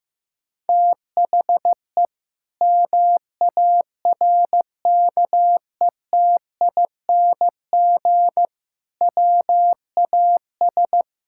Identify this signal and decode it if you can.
Morse code, heard as a keyed tone, about 15 wpm, 700 Hz. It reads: THE MARKETING WAS
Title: Sentences from Top 1000 Words - Review Part 5 - 15wpm